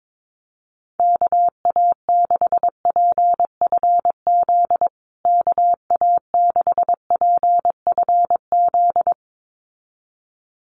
Morse code, heard as a keyed tone, about 22 wpm, 700 Hz. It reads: KA6PFZ KA6PFZ
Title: USA Call Signs Repeated 2x 22wpm